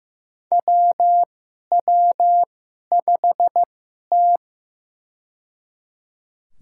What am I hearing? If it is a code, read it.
Morse code, heard as a keyed tone, about 15 wpm, spaced as for 11 wpm, 700 Hz. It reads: WW5T
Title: USA Call Signs - 2x Character Spacing - 15wpm